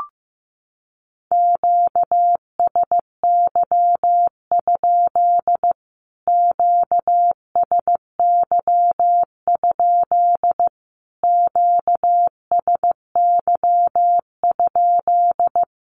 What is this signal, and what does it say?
Morse code, heard as a keyed tone, about 15 wpm, 700 Hz. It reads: QSY? QSY? QSY?
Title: Q-Codes Repeated 3x 15wpm